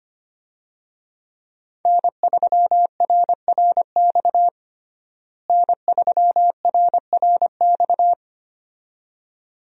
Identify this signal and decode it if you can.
Morse code, heard as a keyed tone, about 25 wpm, 700 Hz. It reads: N3RRX N3RRX
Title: USA Call Signs - 5 Characters - Repeated 2x - 25wpm